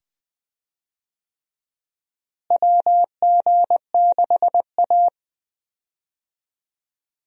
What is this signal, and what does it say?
Morse code, heard as a keyed tone, about 21 wpm, 700 Hz. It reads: WG6A
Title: USA Call Signs - 4 Characters - Repeated 2x - 20wpm